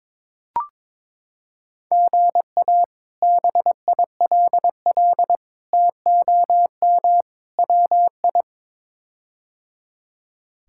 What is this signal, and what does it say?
Morse code, heard as a keyed tone, about 22 wpm, 700 Hz. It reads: GA BILL TOM WI
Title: K1USN SST QSO Practice - 22wpm